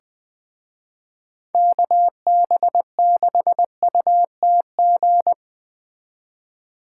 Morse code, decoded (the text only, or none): KB6UTG